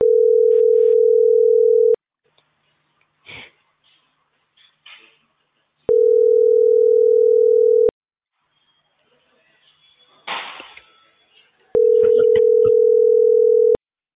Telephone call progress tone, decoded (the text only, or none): ringback tone